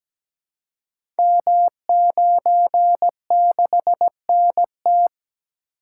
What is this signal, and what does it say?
Morse code, heard as a keyed tone, about 17 wpm, 700 Hz. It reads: M96NT
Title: Code Groups - 5 Letter:Numbers - 17wpm